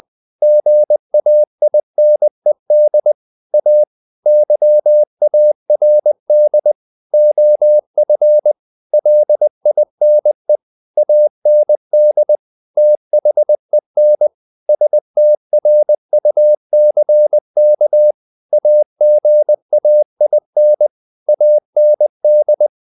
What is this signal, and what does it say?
Morse code, heard as a keyed tone, about 20 wpm, 600 Hz. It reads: GAINED A YARD OF LINE AND THEN STRUCK AGAIN AND
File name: the old man and the sea / the old man and the sea part 1 20wpm